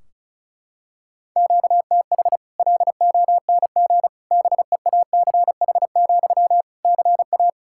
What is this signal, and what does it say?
Morse code, heard as a keyed tone, about 36 wpm, 700 Hz. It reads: QTH LONG BEACH, CA
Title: Practice QSO - Exchange 1 - Beginning - 35wpm